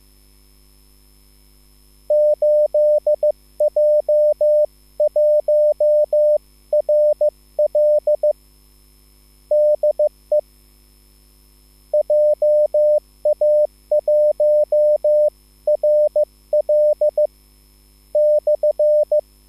Morse code, heard as a keyed tone, about 15 wpm, 600 Hz. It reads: 8J1RL DE JA1RL /